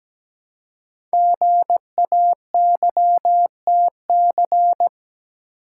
Morse code, heard as a keyed tone, about 17 wpm, 700 Hz. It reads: GAYTC
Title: Code Groups - 5 Letters - 17wpm